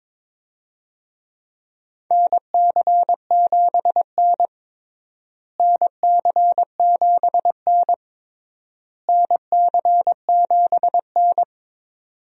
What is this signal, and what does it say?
Morse code, heard as a keyed tone, about 22 wpm, 700 Hz. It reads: NC7N NC7N NC7N
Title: USA Call Signs - 4 Characters - Repeated 3x - 22wpm